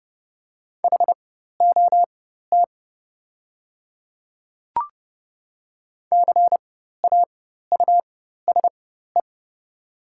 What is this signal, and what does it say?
Morse code, heard as a keyed tone, about 30 wpm, spaced as for 14 wpm, 700 Hz. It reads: HOT CAUSE
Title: 4x Character Spacing - Top 200 Words - 30wpm